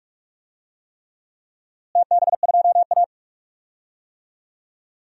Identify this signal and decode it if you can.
Morse code, heard as a keyed tone, about 46 wpm, 700 Hz. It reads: TB2A
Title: DXCC-100 Call Signs - 4 Characters - Repeated 3x - 45wpm